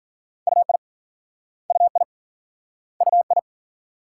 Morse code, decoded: UI UI UI

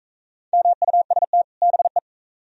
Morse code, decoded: MUST BE

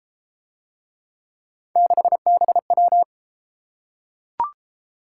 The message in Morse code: 6BW

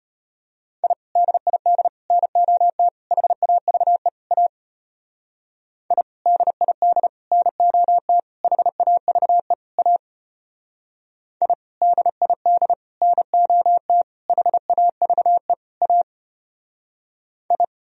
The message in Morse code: I DID NOT HAVE A I DID NOT HAVE A I DID NOT HAVE A I

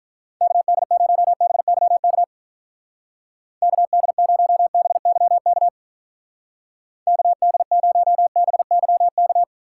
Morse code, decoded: KD0BYK KD0BYK KD0BYK

